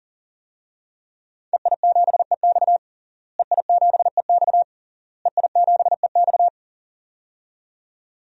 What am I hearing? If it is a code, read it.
Morse code, heard as a keyed tone, about 40 wpm, 700 Hz. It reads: EI7EX EI7EX EI7EX
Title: DXCC-100 Call Signs - 5 Characters - Repeated 3x - 40wpm